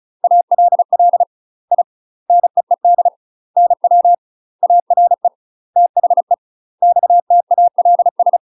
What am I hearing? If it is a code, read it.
Morse code, heard as a keyed tone, about 35 wpm, 700 Hz. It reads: ALL I NEED NW ARE THE XTALS
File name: UK MASTER QSOs TRAINING / 35WPM / UK MASTER QSOs 0043